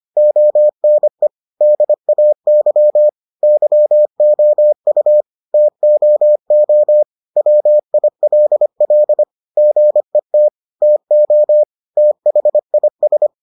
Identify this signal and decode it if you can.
Morse code, heard as a keyed tone, about 25 wpm, 600 Hz. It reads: ONE DAY YOU TOO WILL GET TO THIS